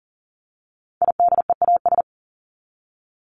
Morse code, heard as a keyed tone, about 40 wpm, 700 Hz. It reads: IDEAS
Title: ES:S Suffix - Top 100 Words with Base Words 40wpm